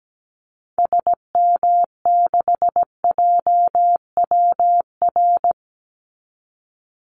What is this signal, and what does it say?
Morse code, heard as a keyed tone, about 17 wpm, 700 Hz. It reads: SM6JWR
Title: Call Signs Repeated 3x - 17wpm